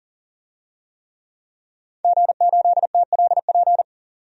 Morse code, heard as a keyed tone, about 40 wpm, 700 Hz. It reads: G8TLP
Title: DXCC-100 Call Signs - 5 Characters - 40wpm